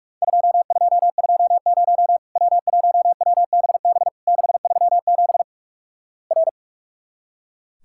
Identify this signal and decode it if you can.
Morse code, heard as a keyed tone, about 44 wpm, 700 Hz. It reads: 2220 W1WBB 637 R
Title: Worked All Europe DX Contest - QTC Exchange - 45wpm